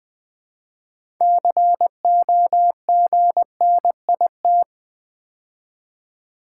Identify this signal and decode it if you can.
Morse code, heard as a keyed tone, about 20 wpm, 700 Hz. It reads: COGNIT
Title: Build A Word - 3 Letters At A Time - 9-Letter Words - 20wpm